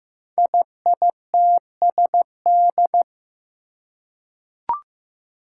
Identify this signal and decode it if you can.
Morse code, heard as a keyed tone, about 15 wpm, 700 Hz. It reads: IITSD
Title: Code Groups - 5 Letters - 15wpm